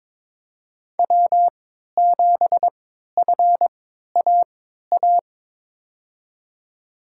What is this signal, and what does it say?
Morse code, heard as a keyed tone, about 21 wpm, spaced as for 12 wpm, 700 Hz. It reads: W7FAA
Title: USA Call Signs - 5 Characters - 3x Character Spacing - 22wpm